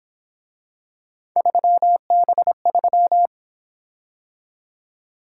Morse code, heard as a keyed tone, about 27 wpm, 700 Hz. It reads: 3B3
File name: Speed-Racing - Three Letter-Number Combinations - 22wpm